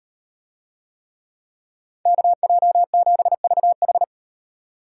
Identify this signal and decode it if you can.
Morse code, heard as a keyed tone, about 38 wpm, 700 Hz. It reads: KJ7VH